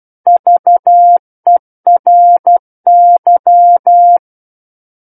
Morse code, heard as a keyed tone, about 12 wpm, 700 Hz. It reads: VERY
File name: Book One / 03-Chapter 3 - 12 WPM